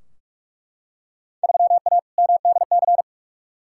Morse code, heard as a keyed tone, about 43 wpm, 700 Hz. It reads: 3A MDC
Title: Field Day Exchanges - 45wpm